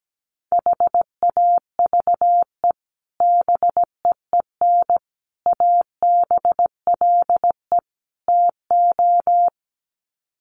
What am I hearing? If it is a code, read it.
Morse code, heard as a keyed tone, about 17 wpm, 700 Hz. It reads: HAVE BEEN ABLE TO
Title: Sets of 4 Words 17wpm